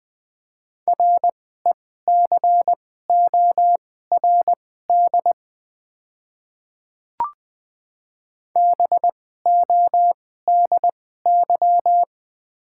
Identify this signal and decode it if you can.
Morse code, heard as a keyed tone, about 20 wpm, spaced as for 14 wpm, 700 Hz. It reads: RECORD BODY
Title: 2x Character Spacing - Top 500 Words - 20wpm